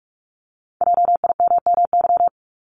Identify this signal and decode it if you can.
Morse code, heard as a keyed tone, about 46 wpm, 700 Hz. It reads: JIMMY
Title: Multi-letter Phonemes 45wpm